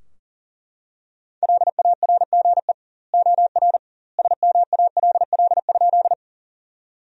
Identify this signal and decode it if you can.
Morse code, heard as a keyed tone, about 40 wpm, 700 Hz. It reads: LARGE OR SMALL?
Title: 2x Word-Spacing - Sentences from Top 200 Words - 40wpm